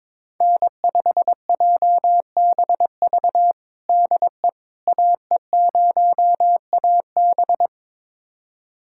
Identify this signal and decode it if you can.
Morse code, heard as a keyed tone, about 21 wpm, 700 Hz. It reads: N5JBV DE AE0AB
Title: USA Callsign DE Callsign - 5 Characters - 22wpm